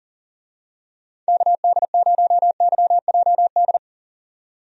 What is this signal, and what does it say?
Morse code, heard as a keyed tone, about 40 wpm, 700 Hz. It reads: KD0YJD
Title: USA Call Signs - 6 Characters - 40wpm